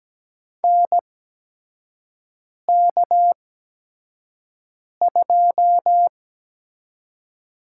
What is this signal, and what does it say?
Morse code, heard as a keyed tone, about 17 wpm, spaced as for 5 wpm, 700 Hz. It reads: NK2
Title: USA Call Signs - 4 Characters - 8x Character Spacing - 17wpm